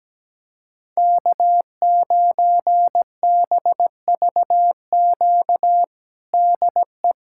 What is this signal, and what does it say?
Morse code, heard as a keyed tone, about 17 wpm, 700 Hz. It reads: K9BVQ DE
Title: USA Callsign DE Callsign - 5 Characters - 17wpm